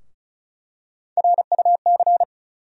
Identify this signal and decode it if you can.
Morse code, heard as a keyed tone, about 36 wpm, 700 Hz. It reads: RUC
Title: USA Call Sign Postfixes - 3 Letters 35wpm